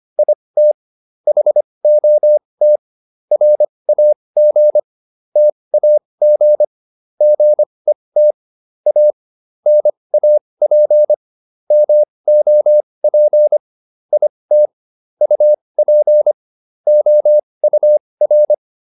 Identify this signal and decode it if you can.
Morse code, heard as a keyed tone, about 25 wpm, spaced as for 20 wpm, 600 Hz. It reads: IT HOT RAG TAG GET A NAP MOP IT UP OUR